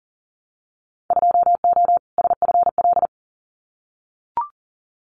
Morse code, heard as a keyed tone, about 38 wpm, 700 Hz. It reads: 2O SFL
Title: Winter Field Day Exchanges 40wpm